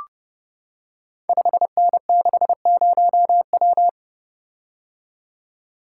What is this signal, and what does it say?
Morse code, heard as a keyed tone, about 30 wpm, 700 Hz. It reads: 5N60W
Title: Code Groups - 5 Letter:Numbers - 30wpm